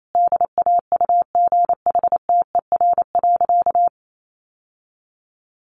Morse code, heard as a keyed tone, about 28 wpm, 700 Hz. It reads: DAUGHTER.
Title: Sentences from Top 1500 Words 28wpm